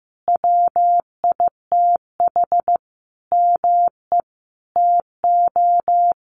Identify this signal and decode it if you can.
Morse code, heard as a keyed tone, about 15 wpm, 700 Hz. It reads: WITH ME TO
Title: Sentences from Top 1300 Words 15wpm